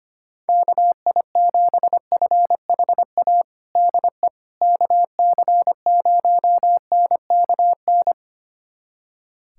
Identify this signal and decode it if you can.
Morse code, heard as a keyed tone, about 25 wpm, 700 Hz. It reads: KI7FHA DE KC0NKN
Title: USA Callsign DE Callsign 25wpm